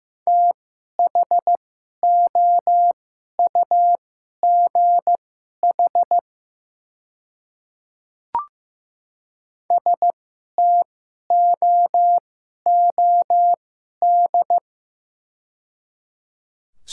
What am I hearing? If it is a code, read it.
Morse code, heard as a keyed tone, about 15 wpm, spaced as for 11 wpm, 690 Hz. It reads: THOUGH STOOD
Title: 2x Character Spacing - Top 500 Words - 15wpm